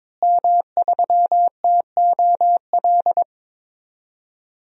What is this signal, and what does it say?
Morse code, heard as a keyed tone, about 22 wpm, 700 Hz. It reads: M3TOL
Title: DXCC-100 Call Signs - 5 Characters - Repeated 3x - 22wpm